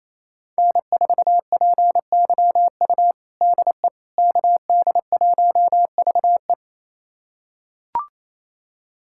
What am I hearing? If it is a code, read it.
Morse code, heard as a keyed tone, about 28 wpm, 700 Hz. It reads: N4PYU DE KD1VE
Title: USA Callsign DE Callsign - 5 Characters - 28wpm